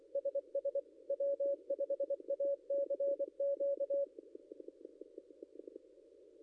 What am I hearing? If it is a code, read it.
Morse code, heard as a keyed tone, about 23 wpm, 560 Hz. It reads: SS W5ACQ
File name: contests / ss-cw-k5tr-op-n5rz -0800